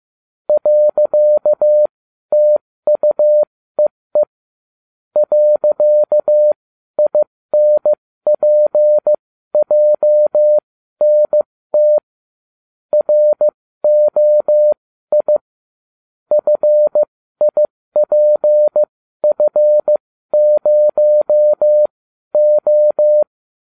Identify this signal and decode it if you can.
Morse code, heard as a keyed tone, about 15 wpm, spaced as for 12 wpm, 600 Hz. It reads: .TUEE .INPJNT ROI FIPF0O